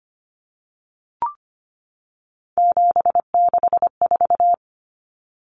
Morse code, heard as a keyed tone, about 25 wpm, 690 Hz. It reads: 764
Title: Numbers - 3-Digit 25wpm